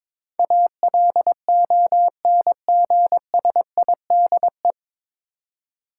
Morse code, heard as a keyed tone, about 22 wpm, 700 Hz. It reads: ALONGSIDE
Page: Morse Code Ninja practice 22wpm